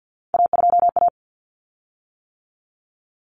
Morse code, heard as a keyed tone, about 48 wpm, 700 Hz. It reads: A2A
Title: Common Cut Numbers - 3-Digit 50wpm